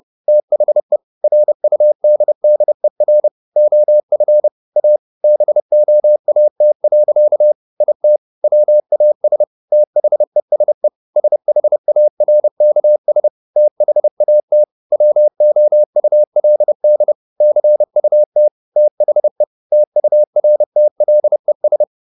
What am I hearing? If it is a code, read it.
Morse code, heard as a keyed tone, about 30 wpm, 600 Hz. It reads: THE RUDDER OF A BOAT. IT WAS THESE SHARKS THAT WOULD CUT THE TURTLES